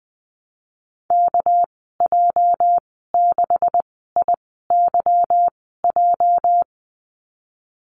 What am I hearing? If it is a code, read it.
Morse code, heard as a keyed tone, about 19 wpm, spaced as for 14 wpm, 700 Hz. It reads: KJ6IYJ
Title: 207 Lesson 33g - Letter j - Call Signs 20wpm